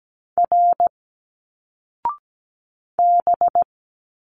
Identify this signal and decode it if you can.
Morse code, heard as a keyed tone, about 17 wpm, 700 Hz. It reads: R B
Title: Single Letters 17wpm